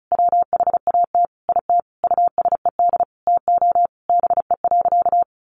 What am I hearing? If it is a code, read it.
Morse code, heard as a keyed tone, about 34 wpm, 700 Hz. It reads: WHAT IT USED TO BE.
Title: Sentences from Top 1100 Words 35wpm